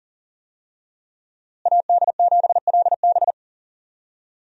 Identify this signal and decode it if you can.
Morse code, heard as a keyed tone, about 40 wpm, 700 Hz. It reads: AD7LB
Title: USA Call Signs - 5 Characters - Repeated 3x - 40wpm